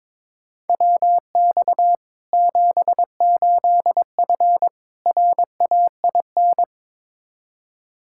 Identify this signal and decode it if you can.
Morse code, heard as a keyed tone, about 21 wpm, 700 Hz. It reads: WX 78F RAIN
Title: K1USN SST QSO Practice 22wpm